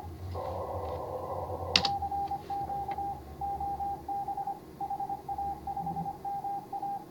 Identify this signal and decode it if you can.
Morse code, heard as a keyed tone, about 24 wpm, 810 Hz. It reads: CQ OB HA5MA